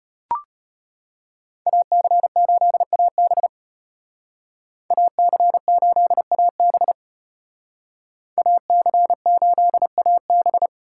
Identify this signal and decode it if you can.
Morse code, heard as a keyed tone, about 33 wpm, 700 Hz. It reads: AC8AB AC8AB AC8AB